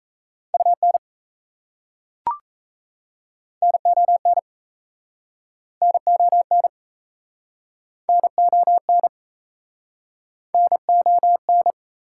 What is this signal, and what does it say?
Morse code, heard as a keyed tone, about 36 wpm, 690 Hz. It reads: UN NON NON NON NON